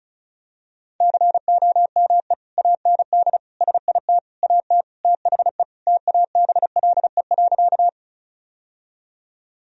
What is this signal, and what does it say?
Morse code, heard as a keyed tone, about 36 wpm, 700 Hz. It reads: COME AND SIT AT THE TABLE.
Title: Sentences from Top 1000 Words - Review Part 2 - 35wpm